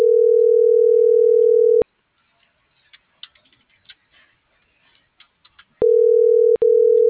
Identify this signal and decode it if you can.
Ringback tone (North America), 440 + 480 Hz.